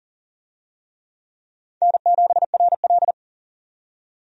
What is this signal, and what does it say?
Morse code, heard as a keyed tone, about 40 wpm, 700 Hz. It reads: N7RL
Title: USA Call Signs - 4 Characters - 40wpm